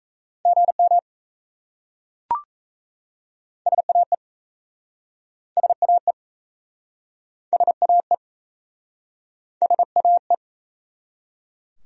Morse code, heard as a keyed tone, about 33 wpm, 700 Hz. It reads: GM SAE SAE SAE SAE
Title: Speed-Racing Abbreviations - 28wpm